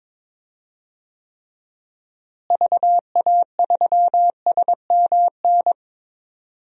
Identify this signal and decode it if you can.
Morse code, heard as a keyed tone, about 22 wpm, 700 Hz. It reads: VA3SMN
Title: DXCC-100 Call Signs - 6 Characters - Repeated 2x - 22wpm